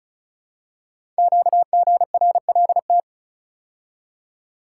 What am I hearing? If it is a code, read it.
Morse code, heard as a keyed tone, about 36 wpm, 690 Hz. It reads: QGRLT
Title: Code Groups - 5 Letters - 35wpm